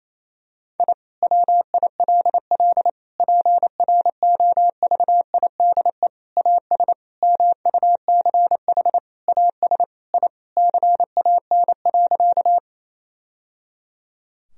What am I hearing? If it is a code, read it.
Morse code, heard as a keyed tone, about 28 wpm, 700 Hz. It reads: I WILL PROVIDE AS MUCH AS I CAN.
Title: Sentences from Top 1000 Words - Review Part 3 - 28wpm